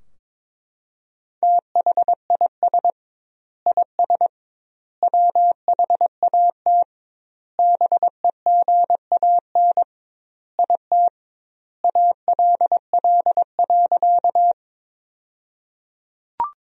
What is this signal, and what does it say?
Morse code, heard as a keyed tone, about 21 wpm, 700 Hz. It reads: THIS IS WHAT BEGAN IT ALL.
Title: 2x Word-Spacing - Sentences from Top 300 Words - 22wpm